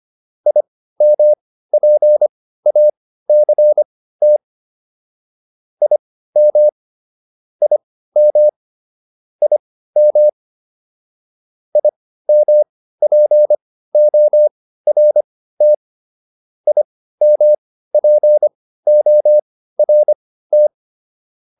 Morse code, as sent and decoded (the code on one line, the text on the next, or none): .. -- .--. .- -.-. - / .. -- / .. -- / .. -- / .. -- .--. --- .-. - / .. -- .--. --- .-. -
IMPACT IM IM IM IMPORT IMPORT